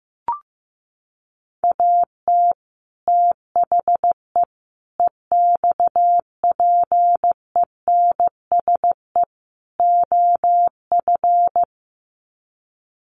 Morse code, heard as a keyed tone, about 15 wpm, 700 Hz. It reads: AT THE EXPENSE OF
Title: Sets of 4 Words 15wpm